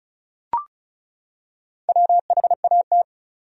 Morse code, heard as a keyed tone, about 34 wpm, 700 Hz. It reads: WHAT
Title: 3x Word-Spacing - Sentences from Top 500 Words - 35wpm